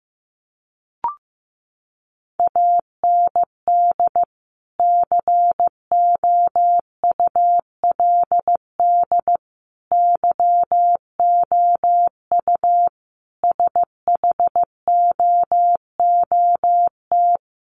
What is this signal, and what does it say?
Morse code, heard as a keyed tone, about 15 wpm, 700 Hz. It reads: AND COULD YOU SHOOT
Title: Sentences from Top 1200 Words 15wpm